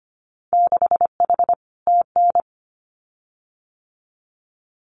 Morse code, heard as a keyed tone, about 25 wpm, 700 Hz. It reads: 6H TN
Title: Winter Field Day Exchanges Encore 25wpm